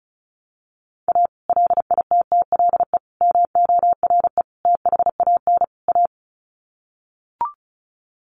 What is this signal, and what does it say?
Morse code, heard as a keyed tone, about 34 wpm, 700 Hz. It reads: A LITTLE MORE THAN A
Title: Sets of 5 Words Encore2 35wpm